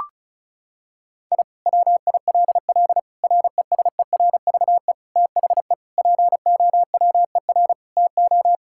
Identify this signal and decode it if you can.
Morse code, heard as a keyed tone, about 36 wpm, 700 Hz. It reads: I WILL RESERVE THE POWER TO
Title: Sentences from Top 1000 Words - Review Part 6 - 35wpm